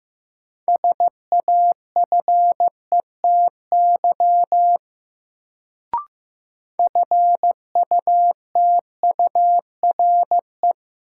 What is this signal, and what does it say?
Morse code, heard as a keyed tone, about 15 wpm, 700 Hz. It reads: SAFETY FUTURE
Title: Sentences from Top 700 Words - New Word Drill - 15wpm